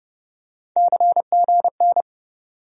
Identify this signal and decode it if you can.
Morse code, heard as a keyed tone, about 30 wpm, 700 Hz. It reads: CGN
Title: USA Call Sign Postfixes - 3 Letters 30wpm